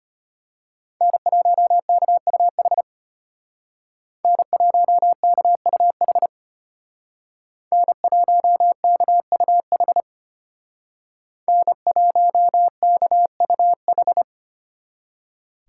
Morse code, heard as a keyed tone, about 30 wpm, 700 Hz. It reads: N1KUH N1KUH N1KUH N1KUH
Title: Speed Racing - USA 1x3 Call Signs 25wpm